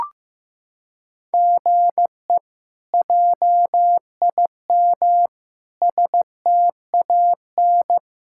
Morse code, heard as a keyed tone, about 15 wpm, 700 Hz. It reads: GE JIM STAN
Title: K1USN SST QSO Practice - 15wpm